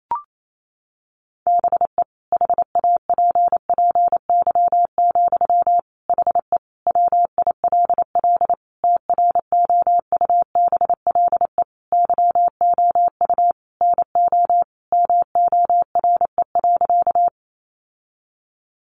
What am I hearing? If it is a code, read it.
Morse code, heard as a keyed tone, about 28 wpm, 700 Hz. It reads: BE HAPPY, HE WILL TROUBLE YOU NO MORE.
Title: Sentences from Top 1200 Words 28wpm